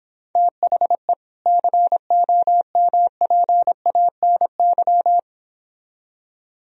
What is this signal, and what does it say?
Morse code, heard as a keyed tone, about 29 wpm, 700 Hz. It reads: THE COMPANY